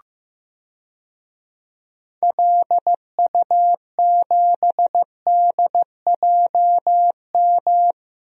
Morse code, heard as a keyed tone, about 15 wpm, 700 Hz. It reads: LU7DJM
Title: DXCC-100 Call Signs - 6 Characters - Repeated 2x - 15wpm